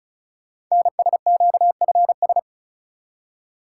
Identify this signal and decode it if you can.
Morse code, heard as a keyed tone, about 36 wpm, 700 Hz. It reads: NSQFS
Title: Code Groups - 5 Letters - 35wpm